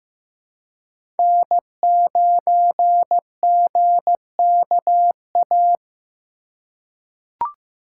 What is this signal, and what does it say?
Morse code, heard as a keyed tone, about 15 wpm, 700 Hz. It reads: N9GKA
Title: USA Call Signs - 5 Characters - 3x Character Spacing - 15wpm